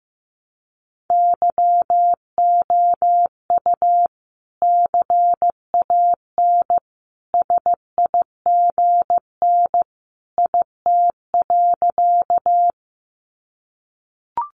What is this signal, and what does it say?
Morse code, heard as a keyed tone, about 15 wpm, 700 Hz. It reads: YOU CAN SIGN IT.
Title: Sentences from Top 1000 Words 15wpm